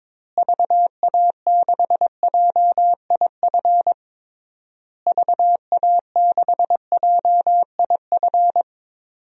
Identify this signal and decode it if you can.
Morse code, heard as a keyed tone, about 22 wpm, 700 Hz. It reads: VA6JIF VA6JIF
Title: DXCC-100 Call Signs - 6 Characters - Repeated 3x - 22wpm